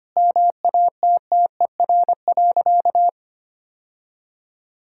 Morse code, heard as a keyed tone, about 25 wpm, 700 Hz. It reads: MATTER.
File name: practice / 25wpm / Sentences with 4-Word Phrases - 25wpm